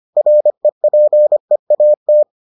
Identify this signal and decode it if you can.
Morse code, heard as a keyed tone, about 25 wpm, 600 Hz. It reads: REPEAT